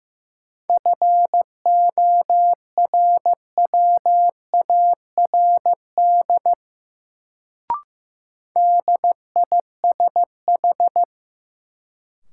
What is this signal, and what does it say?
Morse code, heard as a keyed tone, about 15 wpm, 700 Hz. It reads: FORWARD DISH
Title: Sentences from Top 1300 Words - New Word Drill - 15wpm